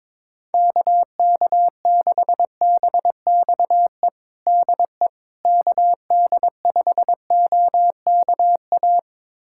Morse code, heard as a keyed tone, about 22 wpm, 700 Hz. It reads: KK6BXE DE KD5OKA
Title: USA Callsign DE Callsign - 6 Characters - 22wpm